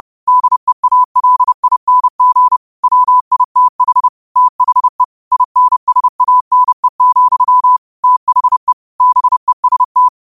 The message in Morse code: DEALING WITH THE INSANE, THE BEST

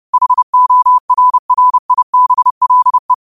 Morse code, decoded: HORRIBLE

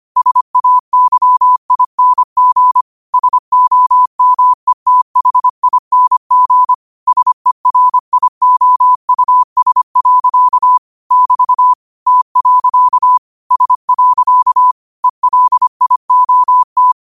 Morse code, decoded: SAYING SOMETHING SERIOUS. <BT> T. S. ELIOT

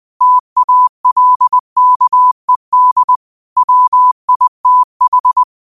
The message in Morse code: TALKED WITH

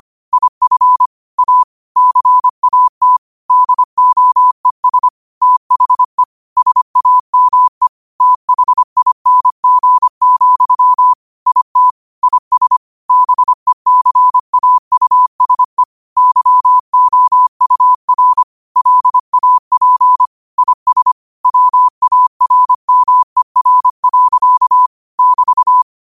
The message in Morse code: IF A CAT DOES THE SAME THING, IT IS BECAUSE YOUR LAP IS WARMER. <BT>

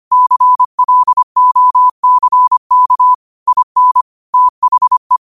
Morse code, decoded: CLOCK IN THE

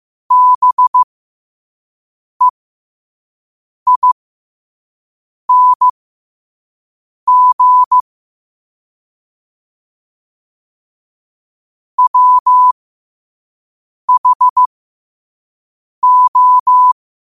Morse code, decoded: BEING WHO

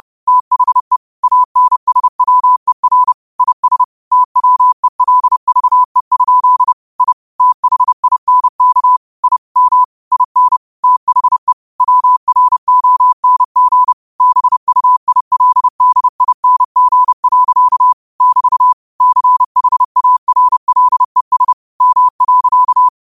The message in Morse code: THE ANSWER IS TWELVE? I THINK I M IN THE WRONG BUILDING. <BT> CHARLES M.

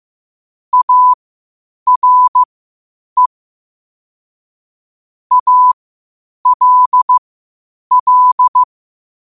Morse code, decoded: ARE ALL

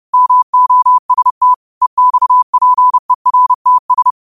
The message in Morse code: MOST EXPERTS

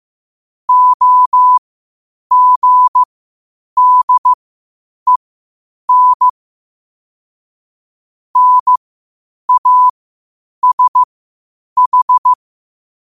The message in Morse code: OGDEN NASH

